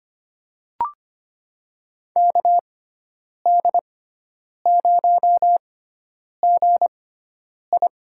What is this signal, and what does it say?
Morse code, heard as a keyed tone, about 25 wpm, spaced as for 9 wpm, 700 Hz. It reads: KD0GI
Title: USA Call Signs - 6x Character Spacing - 25wpm